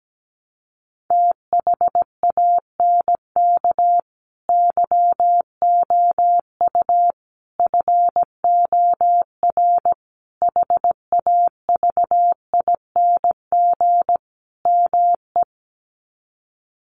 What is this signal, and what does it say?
Morse code, heard as a keyed tone, about 17 wpm, 700 Hz. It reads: THANK YOU FOR HAVING ME
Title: Sets of 5 Words Encore 17wpm